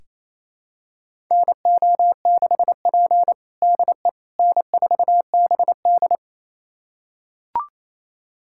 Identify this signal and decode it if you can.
Morse code, heard as a keyed tone, about 27 wpm, 690 Hz. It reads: NO6P DE N4BD
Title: USA Callsign DE Callsign - 4 Characters - 28wpm